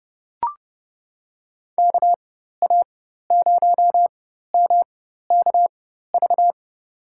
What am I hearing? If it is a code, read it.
Morse code, heard as a keyed tone, about 30 wpm, spaced as for 14 wpm, 700 Hz. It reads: KA0MKV
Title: USA Call Signs - 6 Characters - 4x Character Spacing - 30wpm